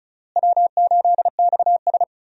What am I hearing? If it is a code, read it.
Morse code, heard as a keyed tone, about 36 wpm, 700 Hz. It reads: W8XS
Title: USA Call Signs - 4 Characters - Repeated 3x - 35wpm